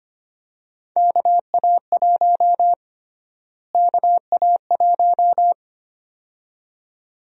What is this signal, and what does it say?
Morse code, heard as a keyed tone, about 25 wpm, 700 Hz. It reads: KA1 KA1
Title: USA Call Sign Prefixes - Repeated 2x - 25wpm